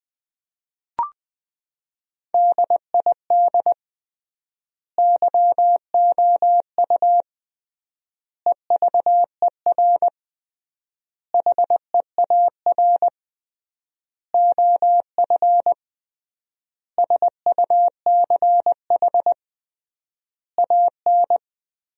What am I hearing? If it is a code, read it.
Morse code, heard as a keyed tone, about 20 wpm, 700 Hz. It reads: DID YOU EVER HEAR OF SUCH AN
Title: 3x Word-Spacing - Sentences from Top 400 Words - 20wpm